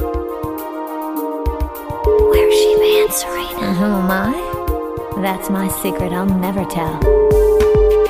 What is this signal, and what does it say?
Ringback tone (Europe, 425 Hz), with speech.